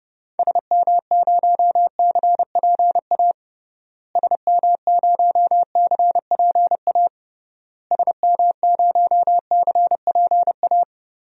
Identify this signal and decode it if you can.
Morse code, heard as a keyed tone, about 32 wpm, 700 Hz. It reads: SM0CPA SM0CPA SM0CPA
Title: DXCC-100 Call Signs - 6 Characters - Repeated 3x - 30wpm